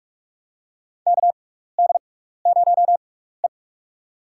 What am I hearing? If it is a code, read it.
Morse code, heard as a keyed tone, about 46 wpm, spaced as for 16 wpm, 700 Hz. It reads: KD0E
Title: USA Call Signs - 4 Characters - 6x Character Spacing - 45wpm